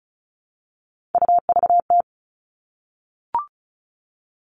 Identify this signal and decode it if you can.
Morse code, heard as a keyed tone, about 34 wpm, 700 Hz. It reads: UVT
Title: All Cut Numbers - 3-Digit 35wpm